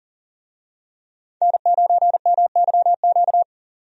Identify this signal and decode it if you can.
Morse code, heard as a keyed tone, about 40 wpm, 700 Hz. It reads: N9MYQ
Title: USA Call Signs - 5 Characters - 40wpm